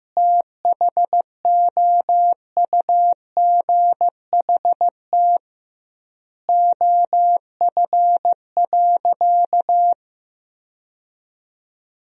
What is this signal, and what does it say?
Morse code, heard as a keyed tone, about 15 wpm, 700 Hz. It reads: THOUGHT OF.
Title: 2x Word-Spacing - Sentences from Top 300 Words - 15wpm